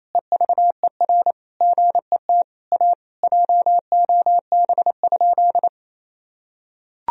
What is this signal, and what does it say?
Morse code, heard as a keyed tone, about 27 wpm, 700 Hz. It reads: EVER GET A JOB?
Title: Sentences from Top 1000 Words - Review Part 3 - 28wpm